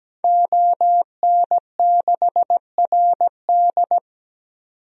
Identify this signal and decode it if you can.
Morse code, heard as a keyed tone, about 17 wpm, 700 Hz. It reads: ON6RD
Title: DXCC-100 Call Signs - 5 Characters - 17wpm